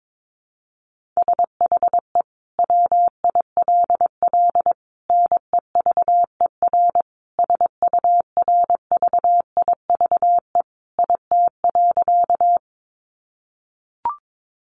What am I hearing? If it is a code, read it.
Morse code, heard as a keyed tone, about 22 wpm, 700 Hz. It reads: SHE WILL NEVER SURVIVE IT.
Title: Sentences from Top 1300 Words 22wpm